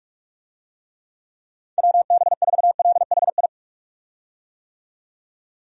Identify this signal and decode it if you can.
Morse code, heard as a keyed tone, about 46 wpm, 700 Hz. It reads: WB4LHI
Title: USA Call Signs - 6 Characters - 45wpm